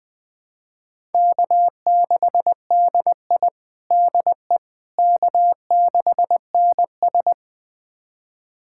Morse code, heard as a keyed tone, about 21 wpm, 700 Hz. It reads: K6DI DE K6NS
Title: USA Callsign DE Callsign - 4 Characters - 20wpm